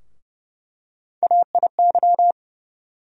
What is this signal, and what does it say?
Morse code, heard as a keyed tone, about 32 wpm, 700 Hz. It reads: AIY
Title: USA Call Sign Postfixes - 3 Letters 30wpm